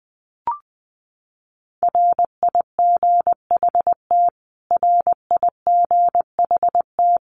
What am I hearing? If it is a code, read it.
Morse code, heard as a keyed tone, about 19 wpm, 700 Hz. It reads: RIGHT RIGHT
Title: Top 500 Words Repeated 20wpm